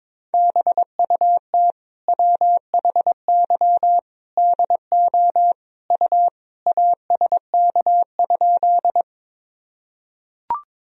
Morse code, heard as a keyed tone, about 22 wpm, 700 Hz. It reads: BUT WHY DO U ASK?
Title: Sentences from Top 200 Words and Abbreviations 22wpm